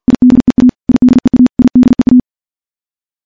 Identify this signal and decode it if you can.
Morse code, heard as a keyed tone, about 49 wpm, 250 Hz. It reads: IDEA IDEA IDEA
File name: morse / morse-training-50WPM